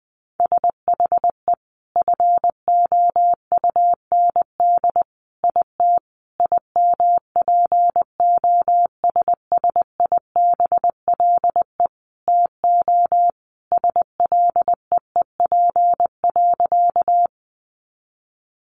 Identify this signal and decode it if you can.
Morse code, heard as a keyed tone, about 20 wpm, 700 Hz. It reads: SHE FOUND IT IMPOSSIBLE TO SLEEP.
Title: Sentences from Top 1100 Words 20wpm